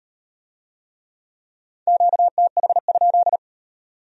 Morse code, heard as a keyed tone, about 38 wpm, 700 Hz. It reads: QTH?